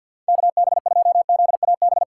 Morse code, heard as a keyed tone, about 52 wpm, 690 Hz. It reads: KB2ZAB
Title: USA Call Signs - 2x Character Spacing - 50wpm